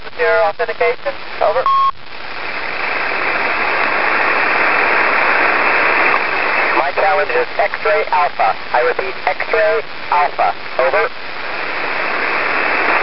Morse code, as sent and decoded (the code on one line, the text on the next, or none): none